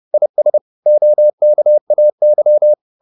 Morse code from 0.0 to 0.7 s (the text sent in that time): IS